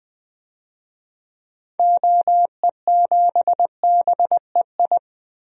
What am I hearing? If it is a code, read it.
Morse code, heard as a keyed tone, about 20 wpm, 700 Hz. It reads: OE7BEI
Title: DXCC-100 Call Signs - 6 Characters - Repeated 2x - 20wpm